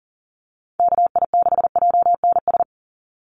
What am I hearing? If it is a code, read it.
Morse code, heard as a keyed tone, about 40 wpm, 700 Hz. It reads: KI6JNS
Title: USA Call Signs 40wpm